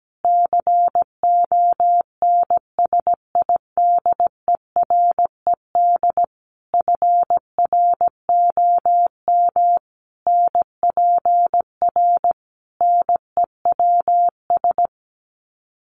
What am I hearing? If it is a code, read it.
Morse code, heard as a keyed tone, about 17 wpm, 700 Hz. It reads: CONSIDERED FROM NPR NEWS
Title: Sets of 5 Words Encore 17wpm